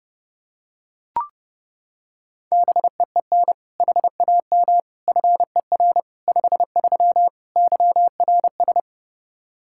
Morse code, heard as a keyed tone, about 30 wpm, 700 Hz. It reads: BEEN HAM FER 53 YRS
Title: Practice QSO - Exchange 3 - More Info - 30wpm